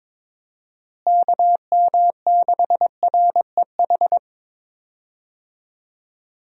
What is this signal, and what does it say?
Morse code, heard as a keyed tone, about 22 wpm, 700 Hz. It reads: KM6REH
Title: USA Call Signs - 6 Characters - Repeated 2x - 22wpm